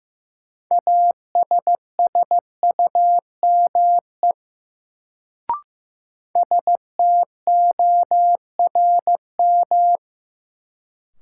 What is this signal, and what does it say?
Morse code, heard as a keyed tone, about 15 wpm, 700 Hz. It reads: ASSUME STORM
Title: Sentences from Top 1500 Words - New Word Drill Alt - 15wpm